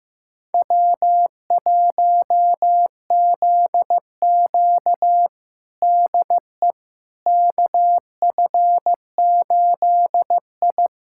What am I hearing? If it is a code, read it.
Morse code, heard as a keyed tone, about 15 wpm, 700 Hz. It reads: W1ZQ DE KF8I
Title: USA Callsign DE Callsign - 4 Characters - 15wpm